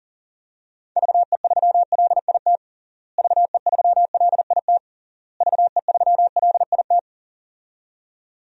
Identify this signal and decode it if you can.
Morse code, heard as a keyed tone, about 40 wpm, 700 Hz. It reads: VE3LIT VE3LIT VE3LIT
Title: DXCC-100 Call Signs - 6 Characters - Repeated 3x - 40wpm